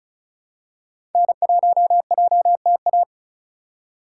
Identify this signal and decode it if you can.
Morse code, heard as a keyed tone, about 34 wpm, 700 Hz. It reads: N1JTA
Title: USA Call Signs - 5 Characters - 3x Character Spacing - 35wpm